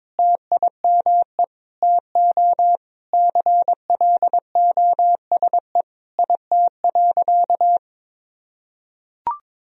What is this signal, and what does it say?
Morse code, heard as a keyed tone, about 22 wpm, 700 Hz. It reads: TIME TO CLOSE IT.
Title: Sentences from Top 1000 Words - Review Part 1 - 22wpm